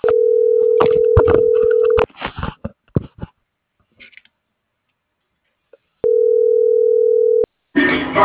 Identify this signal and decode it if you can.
Ringback tone (North America), 440 + 480 Hz.